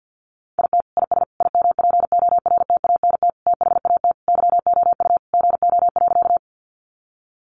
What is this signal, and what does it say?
Morse code, heard as a keyed tone, about 50 wpm, 700 Hz. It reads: IT IS IMPORTANT THAT YOU GO.
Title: Sentences from Top 600 Words 50wpm